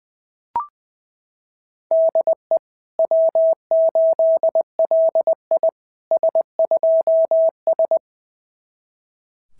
Morse code, heard as a keyed tone, about 20 wpm, 690 Hz. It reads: DE W8LI S2S
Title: SOTA QSO Practice 20wpm